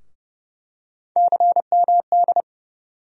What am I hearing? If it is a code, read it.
Morse code, heard as a keyed tone, about 30 wpm, 700 Hz. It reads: CMD
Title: USA Call Sign Postfixes - 3 Letters 30wpm